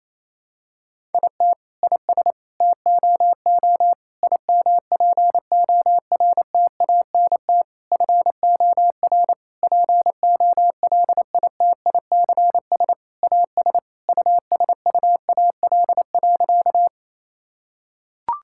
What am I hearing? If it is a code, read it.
Morse code, heard as a keyed tone, about 28 wpm, 700 Hz. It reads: IT IS TOO IMPORTANT FOR POLITICS AS USUAL.
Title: Sentences from Top 1000 Words - Review Part 6 - 28wpm